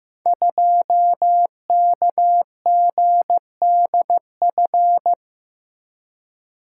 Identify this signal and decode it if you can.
Morse code, heard as a keyed tone, about 15 wpm, 700 Hz. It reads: 2KGDF
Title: Code Groups - 5 Letter:Numbers - 15wpm